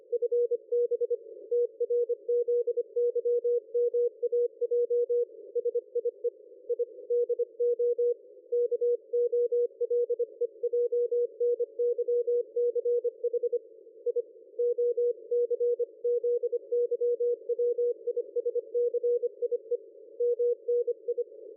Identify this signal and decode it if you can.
Morse code, heard as a keyed tone, about 24 wpm, 480 Hz. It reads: FB TRZYMAJ SIE I DO KOLEJNYCH I OCZYWISCIE MNI
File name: dlazaawansowanych / nagraniazpasma / ExpertSDR2 line out 23.01.2020 14.12.00